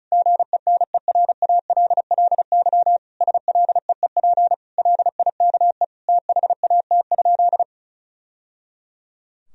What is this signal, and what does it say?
Morse code, heard as a keyed tone, about 34 wpm, 700 Hz. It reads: GENERALLY SLEEP LIKE THAT?
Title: Sentences from Top 1000 Words - Review Part 4 - 35wpm